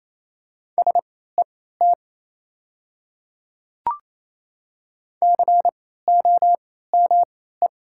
Morse code, heard as a keyed tone, about 28 wpm, spaced as for 16 wpm, 700 Hz. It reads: SET COME
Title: 3x Character Spacing - Top 300 Words - 28wpm